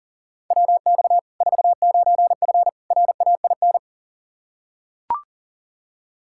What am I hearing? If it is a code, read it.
Morse code, heard as a keyed tone, about 40 wpm, 700 Hz. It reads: WX 49F RAIN
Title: Practice QSO - Exchange 2 - Info - 40wpm